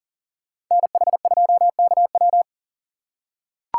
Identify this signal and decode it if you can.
Morse code, heard as a keyed tone, about 40 wpm, 700 Hz. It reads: NH2KW
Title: USA Call Signs - 5 Characters - 6x Character Spacing - 40wpm